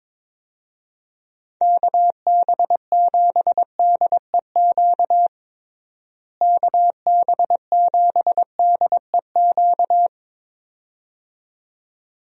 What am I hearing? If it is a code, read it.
Morse code, heard as a keyed tone, about 21 wpm, 700 Hz. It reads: KB7DEQ KB7DEQ
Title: USA Call Signs - 6 Characters - Repeated 2x - 22wpm